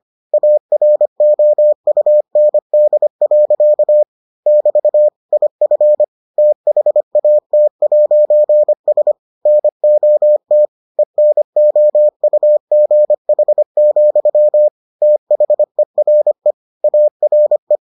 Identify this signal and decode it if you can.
Morse code, heard as a keyed tone, about 25 wpm, 600 Hz. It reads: AROUND. <BT> IF THAT'S NOT ENOUGH, THERE ARE